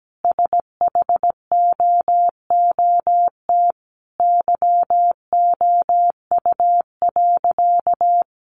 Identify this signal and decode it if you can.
Morse code, heard as a keyed tone, about 17 wpm, 700 Hz. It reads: SHOOT YOU.
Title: Sentences from Top 1200 Words 17wpm